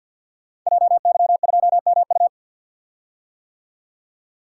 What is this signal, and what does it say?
Morse code, heard as a keyed tone, about 52 wpm, 700 Hz. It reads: JY2MU
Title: Code Groups - 5 Letter:Numbers - 50wpm